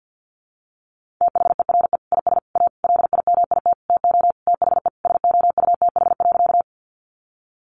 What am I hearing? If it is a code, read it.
Morse code, heard as a keyed tone, about 50 wpm, 700 Hz. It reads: THERE IS A LIMIT TO THE SOUTH.
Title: Sentences from Top 1500 Words 50wpm